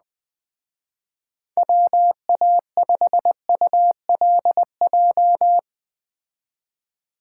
Morse code, heard as a keyed tone, about 20 wpm, 700 Hz. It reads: WA5ULJ
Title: Speed Racing - USA 2x3 Call Signs 17wpm